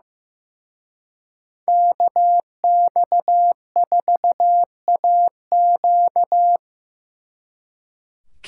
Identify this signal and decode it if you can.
Morse code, heard as a keyed tone, about 18 wpm, 700 Hz. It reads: KX4AQ
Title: Speed Racing - USA 2x2 Call Signs 15wpm